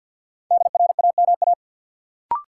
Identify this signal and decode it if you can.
Morse code, heard as a keyed tone, about 50 wpm, 700 Hz. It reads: DRAMA
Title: Sentences from Top 1500 Words - New Word Drill - 50wpm